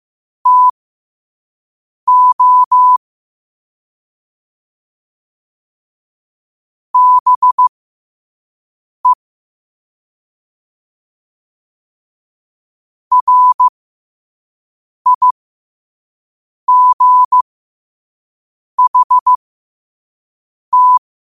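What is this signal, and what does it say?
Morse code, heard as a keyed tone, about 15 wpm, spaced as for 5 wpm, 1000 Hz. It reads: TO BE RIGHT
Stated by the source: Morse code practice at 5 wpm.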